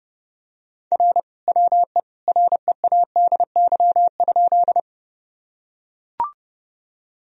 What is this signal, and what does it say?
Morse code, heard as a keyed tone, about 32 wpm, 700 Hz. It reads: R WE READY?
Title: Sentences from Top 400 Words and Abbreviations 30wpm